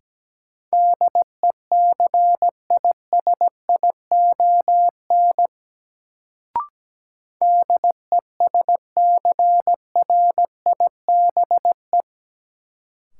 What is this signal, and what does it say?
Morse code, heard as a keyed tone, about 17 wpm, 700 Hz. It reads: DECISION DESCRIBE
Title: Sentences from Top 1000 Words - New Word Drill - 17wpm